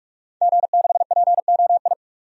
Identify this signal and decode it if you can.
Morse code, heard as a keyed tone, about 46 wpm, 700 Hz. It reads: G6POI
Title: DXCC-100 Call Signs - 5 Characters - 45wpm